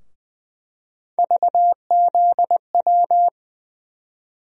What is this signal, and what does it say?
Morse code, heard as a keyed tone, about 20 wpm, 700 Hz. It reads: VZW
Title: USA Call Sign Postfixes - 3 Letters 20wpm